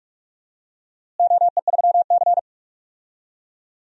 Morse code, heard as a keyed tone, about 44 wpm, 700 Hz. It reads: OE3C
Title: DXCC-100 Call Signs - 4 Characters - Repeated 3x - 45wpm